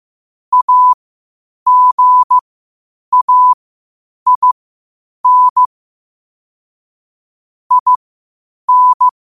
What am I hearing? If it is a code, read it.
Morse code, heard as a keyed tone, about 15 wpm, spaced as for 8 wpm, 1000 Hz. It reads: AGAIN IN
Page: qotd-at-8-wpm